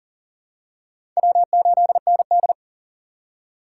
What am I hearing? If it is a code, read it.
Morse code, heard as a keyed tone, about 40 wpm, 700 Hz. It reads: W8ND
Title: USA Call Signs - 4 Characters - 6x Character Spacing - 40wpm